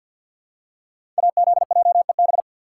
Morse code, heard as a keyed tone, about 45 wpm, 700 Hz. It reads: AZJEB
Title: Code Groups - 5 Letter:Numbers - 50wpm